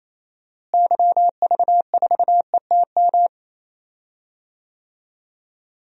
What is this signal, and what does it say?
Morse code, heard as a keyed tone, about 28 wpm, 700 Hz. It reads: YV4ETM
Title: DXCC-100 Call Signs - 6 Characters - Repeated 3x - 28wpm